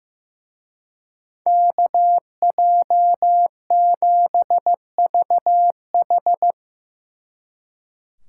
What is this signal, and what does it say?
Morse code, heard as a keyed tone, about 15 wpm, 700 Hz. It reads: KJ7VH
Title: Speed Racing - USA 2x2 Call Signs 15wpm